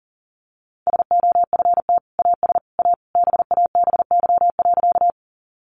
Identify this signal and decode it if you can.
Morse code, heard as a keyed tone, about 40 wpm, 700 Hz. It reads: SOFT AS A BABY.
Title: Sentences from Top 1500 Words 40wpm